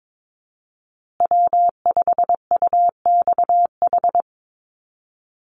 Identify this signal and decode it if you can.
Morse code, heard as a keyed tone, about 22 wpm, 700 Hz. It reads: W5UXH
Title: Call Signs Repeated 2x - 22wpm